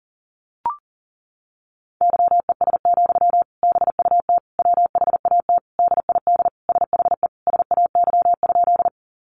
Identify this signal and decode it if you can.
Morse code, heard as a keyed tone, about 37 wpm, 700 Hz. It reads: YES, BUT WHAT DID SHE SAY?
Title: Sentences from Top 500 Words 40wpm